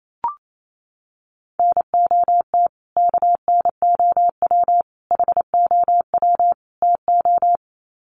Morse code, heard as a keyed tone, about 28 wpm, 700 Hz. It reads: NOT KNOW HOW TO
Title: Sets of 4 Words 28wpm